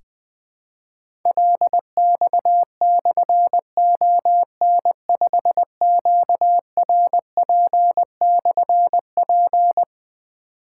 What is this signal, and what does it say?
Morse code, heard as a keyed tone, about 20 wpm, 700 Hz. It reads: LX/ON5QRP/P
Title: Difficult Call Signs 20wpm